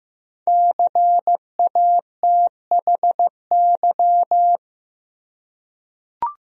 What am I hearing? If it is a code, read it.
Morse code, heard as a keyed tone, about 15 wpm, 700 Hz. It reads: CATHY
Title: USA Common Names - Max 5 Letters - 15wpm